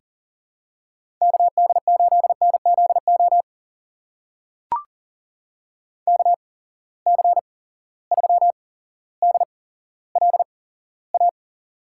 Morse code, heard as a keyed tone, about 40 wpm, spaced as for 17 wpm, 700 Hz. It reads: KD8NZO KC3DLA
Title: USA Call Signs - 6 Characters - 8x Character Spacing - 40wpm